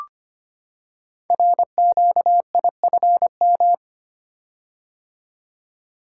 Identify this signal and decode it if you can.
Morse code, heard as a keyed tone, about 25 wpm, 700 Hz. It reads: RQIFM
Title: Code Groups - 5 Letters - 25wpm